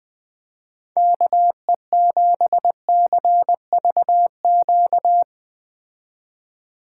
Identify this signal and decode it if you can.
Morse code, heard as a keyed tone, about 20 wpm, 700 Hz. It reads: KE7CVQ